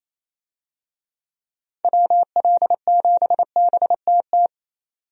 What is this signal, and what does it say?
Morse code, heard as a keyed tone, about 28 wpm, 700 Hz. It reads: WL7BTT
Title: USA Call Signs - 6 Characters - 28wpm